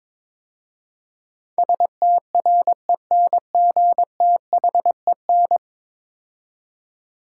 Morse code, heard as a keyed tone, about 23 wpm, 700 Hz. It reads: STRENGTHEN
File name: practice / 22wpm / 10 Letter Words - First 3 Letters Given - 22wpm